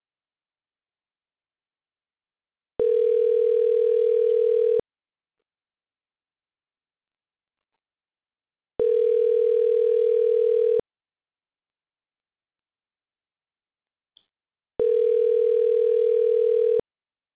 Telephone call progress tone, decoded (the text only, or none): ringback tone